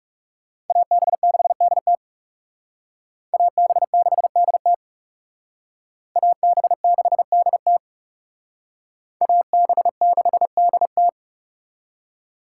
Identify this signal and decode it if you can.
Morse code, heard as a keyed tone, about 39 wpm, 700 Hz. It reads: AB6DT AB6DT AB6DT AB6DT